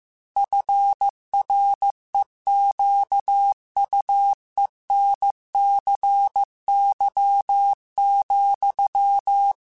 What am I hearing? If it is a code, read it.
Morse code, heard as a keyed tone, about 15 wpm, 800 Hz. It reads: FREQUENCY,